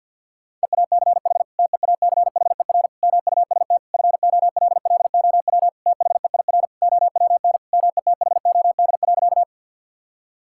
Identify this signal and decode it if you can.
Morse code, heard as a keyed tone, about 50 wpm, 700 Hz. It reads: EACH TEACHER MUST FOLLOW THEIR OWN METHOD.
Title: Sentences from Top 1000 Words - Review Part 6 - 50wpm